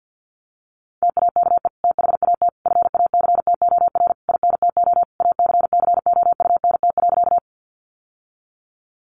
Morse code, heard as a keyed tone, about 50 wpm, 700 Hz. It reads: TAKE THAT FACTOR INTO ACCOUNT.
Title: Sentences from Top 1300 Words 50wpm